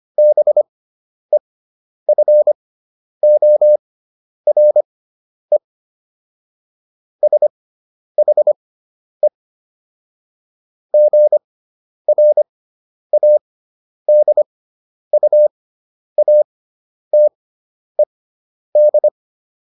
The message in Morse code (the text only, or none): BEFORE SHE GRADUATED